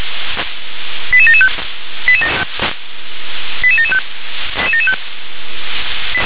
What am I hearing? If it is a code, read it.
Five-tone selective call (ZVEI), heard as quick runs of five tones, tones of about 70 ms. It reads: 88775 88775
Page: SDR/ZVEI